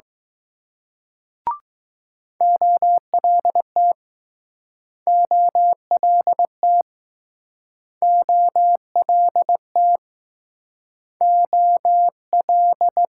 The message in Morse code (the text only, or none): OLT OLT OLT O L